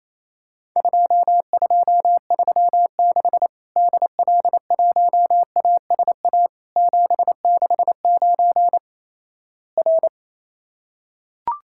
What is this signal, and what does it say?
Morse code, heard as a keyed tone, about 28 wpm, 700 Hz. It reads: 2236 DL1ASA 769 R
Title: Worked All Europe DX Contest - QTC Exchange - 28wpm